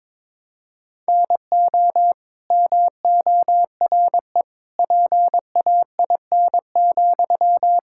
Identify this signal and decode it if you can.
Morse code, heard as a keyed tone, about 22 wpm, 700 Hz. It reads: NO MORE PAIN,
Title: Sentences from Top 1600 Words 22wpm